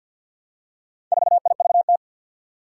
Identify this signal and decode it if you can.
Morse code, heard as a keyed tone, about 50 wpm, 700 Hz. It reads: 4IVT
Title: Four Challenging Letter-Number Combinations 50wpm